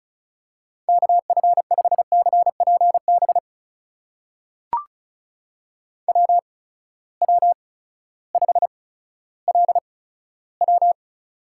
Morse code, heard as a keyed tone, about 36 wpm, spaced as for 16 wpm, 700 Hz. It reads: KF5CPB WW5LW
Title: USA Call Signs - 8x Character Spacing - 35wpm